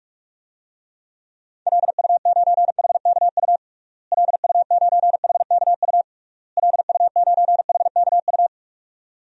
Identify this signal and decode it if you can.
Morse code, heard as a keyed tone, about 44 wpm, 700 Hz. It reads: LU9HKU LU9HKU LU9HKU
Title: DXCC-100 Call Signs - 6 Characters - Repeated 3x - 45wpm